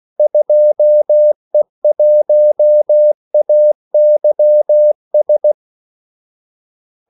Morse code, heard as a keyed tone, about 16 wpm, 600 Hz. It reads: 2E1AYS
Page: Kaldesignaler 16 wpm